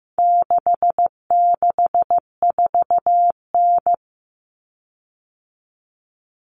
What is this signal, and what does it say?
Morse code, heard as a keyed tone, about 15 wpm, 700 Hz. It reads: 664N
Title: Common Cut Numbers - 4-Digit 15wpm